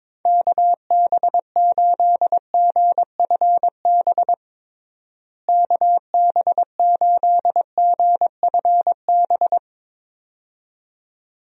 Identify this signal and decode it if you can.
Morse code, heard as a keyed tone, about 22 wpm, 700 Hz. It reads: KB8GFB KB8GFB
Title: USA Call Signs - 6 Characters - Repeated 2x - 22wpm